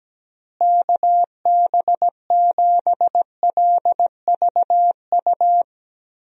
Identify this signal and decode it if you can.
Morse code, heard as a keyed tone, about 17 wpm, 700 Hz. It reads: KB7LVU
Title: USA Call Signs - 6 Characters - Repeated 3x - 17wpm